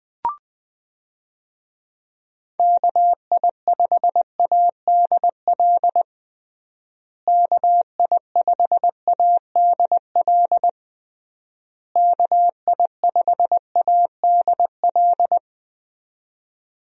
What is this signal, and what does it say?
Morse code, heard as a keyed tone, about 20 wpm, 700 Hz. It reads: KI5ADL KI5ADL KI5ADL
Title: USA Call Signs - 6 Characters - Repeated 3x - 20wpm